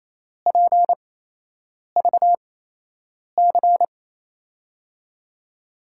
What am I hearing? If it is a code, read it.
Morse code, heard as a keyed tone, about 28 wpm, spaced as for 8 wpm, 700 Hz. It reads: PVC
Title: USA Call Signs - 6 Characters - 8x Character Spacing - 28wpm